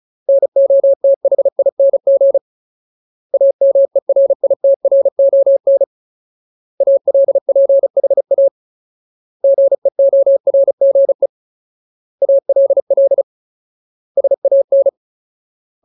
Morse code, heard as a keyed tone, about 34 wpm, 550 Hz. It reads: NOTHING AMERITRON ALPHA GEORGE ALL SAN
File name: Ultimate Word List / 35WPM Ultimate